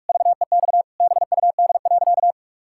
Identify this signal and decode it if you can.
Morse code, heard as a keyed tone, about 46 wpm, 700 Hz. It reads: VEX BUD.
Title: Pangram Challenge v2 - 45wpm